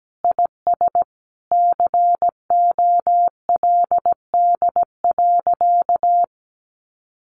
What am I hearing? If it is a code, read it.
Morse code, heard as a keyed tone, about 17 wpm, 700 Hz. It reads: IS COLD.
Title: Sentences from Top 1200 Words 17wpm